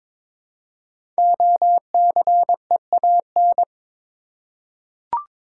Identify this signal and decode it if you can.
Morse code, heard as a keyed tone, about 22 wpm, 700 Hz. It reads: OCEAN
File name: 5 Letter Words - First 3 Letters Given - 22wpm